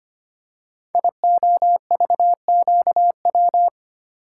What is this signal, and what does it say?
Morse code, heard as a keyed tone, about 25 wpm, 700 Hz. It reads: IOVQW
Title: Code Groups - 5 Letter:Numbers - 25wpm